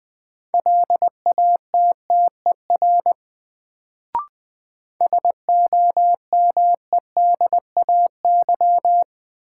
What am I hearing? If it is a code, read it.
Morse code, heard as a keyed tone, about 20 wpm, 700 Hz. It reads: LATTER SOMEDAY
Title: Sentences from Top 1600 Words - New Word Drill - 20wpm